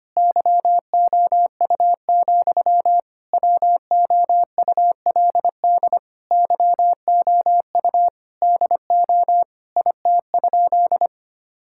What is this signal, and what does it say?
Morse code, heard as a keyed tone, about 25 wpm, 700 Hz. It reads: YOU, WOULD YOU DO IT?
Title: Sentences from Top 1000 Words - Review Part 5 - 25wpm